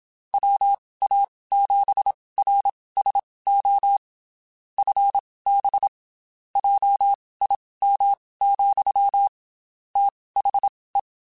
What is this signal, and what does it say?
Morse code, heard as a keyed tone, about 27 wpm, spaced as for 19 wpm, 800 Hz. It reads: WA7RSO FB JIM, THE
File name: QSOs / QSOs-mp3s / WD0BGP 1@25